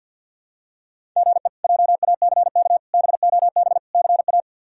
Morse code, heard as a keyed tone, about 47 wpm, 700 Hz. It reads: GE JACK BOB CA
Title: K1USN SST QSO Practice - 50wpm